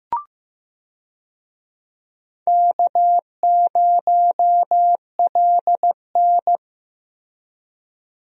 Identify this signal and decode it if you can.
Morse code, heard as a keyed tone, about 15 wpm, 700 Hz. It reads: K0LN
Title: USA Call Signs - 4 Characters - Repeated 3x - 15wpm